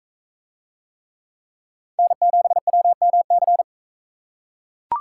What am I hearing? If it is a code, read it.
Morse code, heard as a keyed tone, about 40 wpm, 700 Hz. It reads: N7WMC